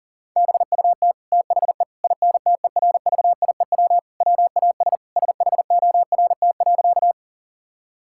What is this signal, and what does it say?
Morse code, heard as a keyed tone, about 37 wpm, 700 Hz. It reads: BUT THE INTERVIEW WAS SHORT.
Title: Sentences from Top 1600 Words 40wpm